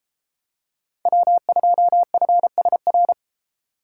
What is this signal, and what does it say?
Morse code, heard as a keyed tone, about 32 wpm, 700 Hz. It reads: W2FSR